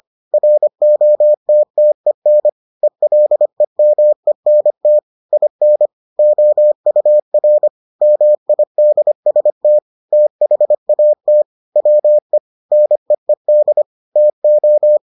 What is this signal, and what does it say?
Morse code, heard as a keyed tone, about 25 wpm, 600 Hz. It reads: ROTTEN ELEMENT IN OUR MIDST THAT WE NEED TO